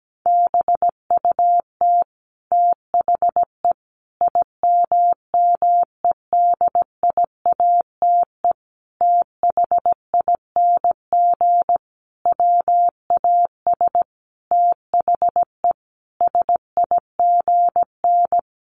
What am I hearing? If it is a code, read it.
Morse code, heard as a keyed tone, about 17 wpm, 700 Hz. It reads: BUT THE IMMEDIATE THING WAS THE SIGN
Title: A Sound Of Thunder 17wpm